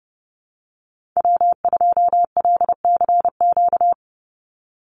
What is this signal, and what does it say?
Morse code, heard as a keyed tone, about 30 wpm, 700 Hz. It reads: W2LCQ
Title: Call Signs - 30wpm